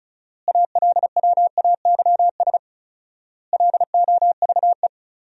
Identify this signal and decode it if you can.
Morse code, heard as a keyed tone, about 34 wpm, 700 Hz. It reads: ALWAYS LOVE
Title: 4x Word-Spacing - Sentences from Top 400 Words - 35wpm